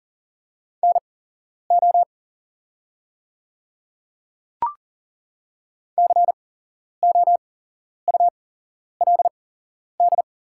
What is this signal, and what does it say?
Morse code, heard as a keyed tone, about 40 wpm, spaced as for 11 wpm, 700 Hz. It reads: NO COULD
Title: 8x Character Spacing - Top 100 Words - 40wpm